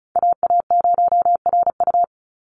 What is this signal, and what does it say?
Morse code, heard as a keyed tone, about 34 wpm, 700 Hz. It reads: AA0RU
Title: USA Call Signs 35wpm